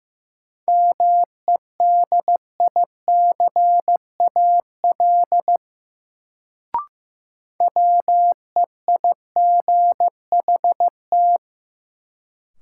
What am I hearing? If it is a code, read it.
Morse code, heard as a keyed tone, about 15 wpm, 700 Hz. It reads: MEDICAL WEIGHT
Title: Sentences from Top 800 Words - New Word Drill - 15wpm